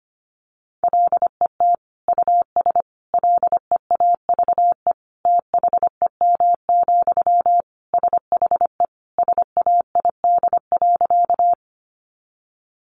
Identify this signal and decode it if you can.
Morse code, heard as a keyed tone, about 25 wpm, 700 Hz. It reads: LET US LEAVE THEM, SHE SAID.
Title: Sentences from Top 400 Words 25wpm